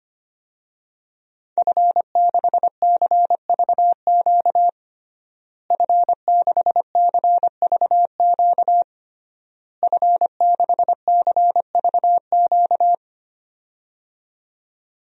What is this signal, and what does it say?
Morse code, heard as a keyed tone, about 25 wpm, 700 Hz. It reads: F6CVQ F6CVQ F6CVQ
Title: DXCC-100 Call Signs - 5 Characters - Repeated 3x - 25wpm